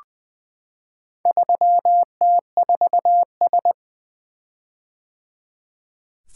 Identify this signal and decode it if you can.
Morse code, heard as a keyed tone, about 20 wpm, 700 Hz. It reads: 3T4S
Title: Four Letter-Number Combinations - 20wpm